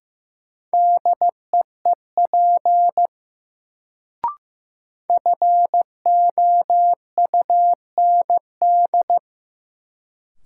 Morse code, heard as a keyed tone, about 15 wpm, 700 Hz. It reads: DEEP FOUND
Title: Sets of 5 Words Encore - New Word Drill - 15wpm